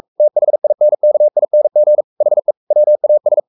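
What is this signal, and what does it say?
Morse code, heard as a keyed tone, about 44 wpm, 600 Hz. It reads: THINKING HE WAS
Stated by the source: Morse code at 43 wpm sending Ken Follett's novel Never in English, from CW med Gnister.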